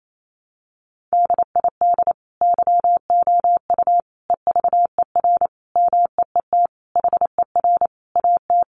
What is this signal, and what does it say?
Morse code, heard as a keyed tone, about 28 wpm, 700 Hz. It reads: DID YOU EVER MEET HER AT
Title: Sentences from Top 1100 Words 28wpm